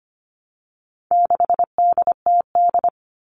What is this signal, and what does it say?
Morse code, heard as a keyed tone, about 25 wpm, 700 Hz. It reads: 6DTD
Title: All Cut Numbers - 4-Digit 25wpm